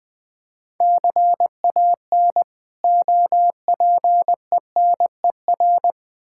Morse code, heard as a keyed tone, about 20 wpm, 690 Hz. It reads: CAN OPENER
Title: Difficult Words with Hints - 20wpm